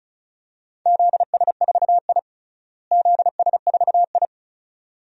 Morse code, heard as a keyed tone, about 36 wpm, 700 Hz. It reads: ZS4I ZS4I
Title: DXCC-100 Call Signs - 4 Characters - Repeated 2x - 35wpm